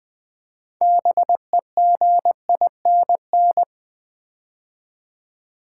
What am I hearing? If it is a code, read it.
Morse code, heard as a keyed tone, about 20 wpm, 700 Hz. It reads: BEGINN
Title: Build A Word - 3 Letters At A Time - 9-Letter Words - 20wpm